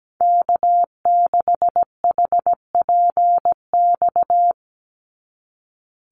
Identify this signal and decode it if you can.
Morse code, heard as a keyed tone, about 17 wpm, 700 Hz. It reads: K6HPX
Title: NA SOTA Activator Callsigns 17wpm